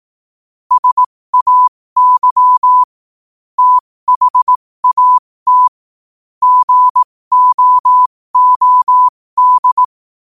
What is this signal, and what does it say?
Morse code, heard as a keyed tone, about 18 wpm, spaced as for 15 wpm, 1000 Hz. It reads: SAY THAT GOOD
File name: qotdfiles / 15wpm